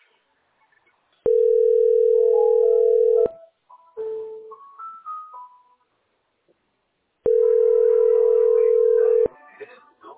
Ringback tone (North America), 440 + 480 Hz.